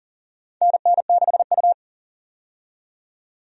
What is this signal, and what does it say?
Morse code, heard as a keyed tone, about 40 wpm, 700 Hz. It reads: NN6U
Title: USA Call Signs - 4 Characters - 40wpm